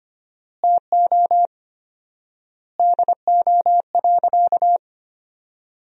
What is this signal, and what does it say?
Morse code, heard as a keyed tone, about 25 wpm, 700 Hz. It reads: TO DO.
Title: 4x Word-Spacing - Sentences from Top 100 Words - 25wpm